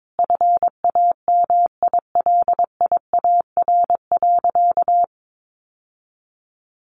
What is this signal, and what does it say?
Morse code, heard as a keyed tone, about 22 wpm, 700 Hz. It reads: FAMILIAR.
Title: Sentences from Top 1000 Words 22wpm